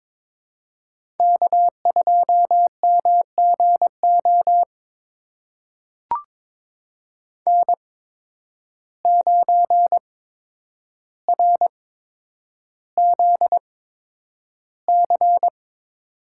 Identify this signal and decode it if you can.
Morse code, heard as a keyed tone, about 21 wpm, spaced as for 9 wpm, 700 Hz. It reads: K2MGO N9RZC
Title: USA Call Signs - 5 Characters - 8x Character Spacing - 22wpm